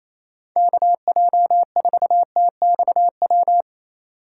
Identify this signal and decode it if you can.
Morse code, heard as a keyed tone, about 28 wpm, 700 Hz. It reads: KJ4TXW
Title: USA Call Signs - 4x Character Spacing - 28wpm